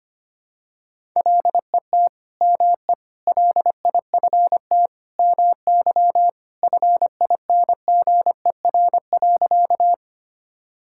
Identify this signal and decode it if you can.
Morse code, heard as a keyed tone, about 25 wpm, 700 Hz. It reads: LET ME LIFT MY FINGER.
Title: Sentences from Top 1600 Words 25wpm